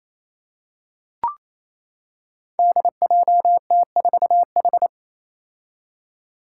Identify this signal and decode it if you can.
Morse code, heard as a keyed tone, about 28 wpm, 700 Hz. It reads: DJT4H
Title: Five Challenging Letter-Number Combinations 28wpm